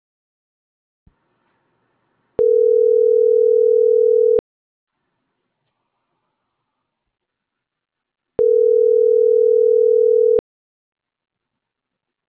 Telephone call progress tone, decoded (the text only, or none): ringback tone